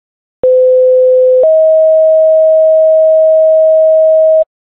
Two-tone page, heard as a one-second tone, then a three-second tone, about 510 then 640 Hz. It is heard from 0.4 to 4.4 s.
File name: 19duty